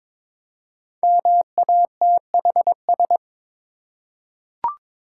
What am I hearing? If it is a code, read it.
Morse code, heard as a keyed tone, about 22 wpm, 700 Hz. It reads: MATHS
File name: 5 Letter Words - First 3 Letters Given - 22wpm